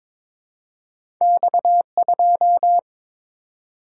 Morse code, heard as a keyed tone, about 22 wpm, 700 Hz. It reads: X2
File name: Two Letter-Number Combinations - 22wpm